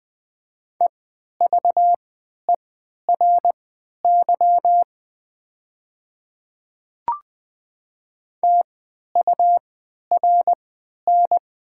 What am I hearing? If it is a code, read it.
Morse code, heard as a keyed tone, about 20 wpm, spaced as for 11 wpm, 700 Hz. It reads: EVERY TURN
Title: 3x Character Spacing - Top 500 Words - 20wpm